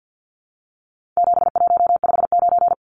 Morse code, heard as a keyed tone, about 50 wpm, 700 Hz. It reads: 7159
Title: Numbers - 4-Digit 50wpm